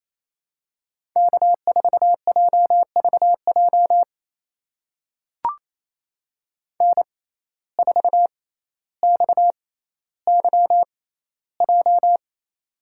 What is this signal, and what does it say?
Morse code, heard as a keyed tone, about 28 wpm, spaced as for 14 wpm, 700 Hz. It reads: K4JVJ N4XYJ
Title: USA Call Signs - 6x Character Spacing - 28wpm